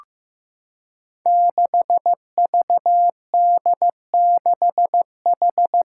Morse code, heard as a keyed tone, about 15 wpm, 700 Hz. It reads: 6VD6H
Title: Five Challenging Letter-Number Combinations 15wpm